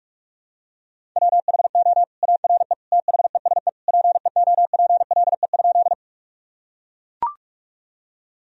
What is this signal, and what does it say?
Morse code, heard as a keyed tone, about 41 wpm, 700 Hz. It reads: WHO ARE THESE PEOPLE?
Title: Sentences from Top 1000 Words - Review Part 1 - 45wpm